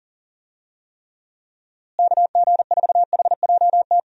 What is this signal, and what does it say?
Morse code, heard as a keyed tone, about 40 wpm, 700 Hz. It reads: KG4HJT
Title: USA Call Signs - 6 Characters - Repeated 3x - 40wpm